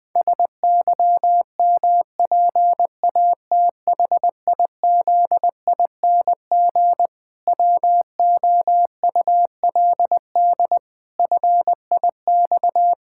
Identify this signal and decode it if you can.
Morse code, heard as a keyed tone, about 20 wpm, 700 Hz. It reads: SYMPATHIZING WOULD FIX
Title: Pangram Challenge v2 - 20wpm